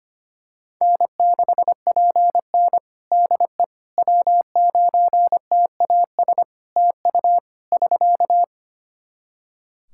Morse code, heard as a keyed tone, about 25 wpm, 700 Hz. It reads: N6PN DE W9TAS TU <SK>
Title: Practice QSO - Exchange 4 - Ending - 25wpm